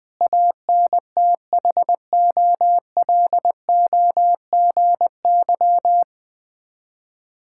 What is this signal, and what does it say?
Morse code, heard as a keyed tone, about 20 wpm, 690 Hz. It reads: ANTHOLOGY
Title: Difficult Words with Hints - 20wpm